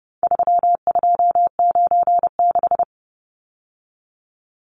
Morse code, heard as a keyed tone, about 29 wpm, 700 Hz. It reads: 3296
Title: Numbers - 4-Digit 30wpm